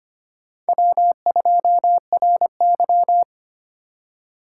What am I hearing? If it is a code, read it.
Morse code, heard as a keyed tone, about 32 wpm, 700 Hz. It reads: W2RY